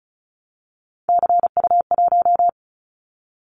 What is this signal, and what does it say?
Morse code, heard as a keyed tone, about 34 wpm, 700 Hz. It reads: CU1
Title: Callsign Prefixes for Top 100 DXCC - 35wpm